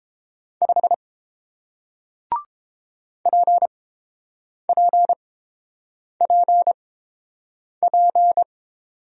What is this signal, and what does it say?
Morse code, heard as a keyed tone, about 31 wpm, spaced as for 7 wpm, 700 Hz. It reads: 5 PPPP